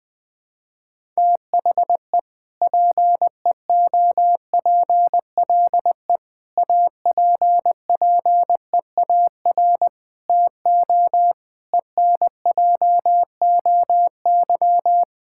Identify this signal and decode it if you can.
Morse code, heard as a keyed tone, about 21 wpm, 700 Hz. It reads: THE PEOPLE APPEAR TO ENJOY
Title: Sentences from Top 1000 Words - Review Part 5 - 20wpm